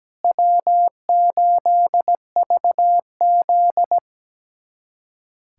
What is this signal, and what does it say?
Morse code, heard as a keyed tone, about 17 wpm, 700 Hz. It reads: W8VZ
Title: USA Callsign DE Callsign - 4 Characters - 17wpm